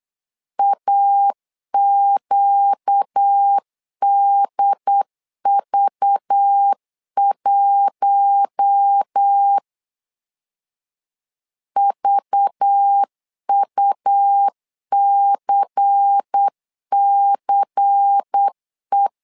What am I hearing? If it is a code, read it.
Morse code, heard as a keyed tone, about 8 wpm, 800 Hz. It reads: AQDV1 VUCCE
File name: LEZIONI / 5 LBVC / 5 LBVC 8wpm 2935